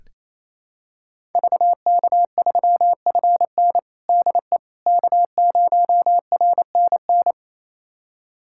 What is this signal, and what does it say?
Morse code, heard as a keyed tone, about 28 wpm, 700 Hz. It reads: VK3FN DE K0RNN
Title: USA Callsign DE Callsign - 5 Characters - 28wpm